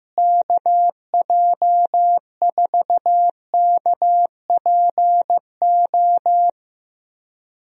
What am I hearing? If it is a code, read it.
Morse code, heard as a keyed tone, about 15 wpm, 700 Hz. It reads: KJ4KPO
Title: USA Call Signs - 6 Characters - Repeated 2x - 15wpm